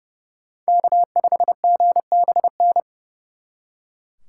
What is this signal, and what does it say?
Morse code, heard as a keyed tone, about 37 wpm, 700 Hz. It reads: K5GBN
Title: Speed Racing - USA 1x3 Call Signs 30wpm